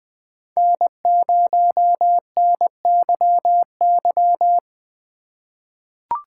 Morse code, heard as a keyed tone, about 20 wpm, 700 Hz. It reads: N0NYY
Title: USA Call Signs - 5 Characters - 2x Character Spacing - 20wpm